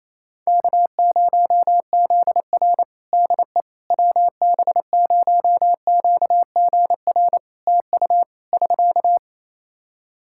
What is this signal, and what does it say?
Morse code, heard as a keyed tone, about 28 wpm, 700 Hz. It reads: K0ZR DE WB0QGR TU <SK>
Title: Practice QSO - Exchange 4 - Ending - 28wpm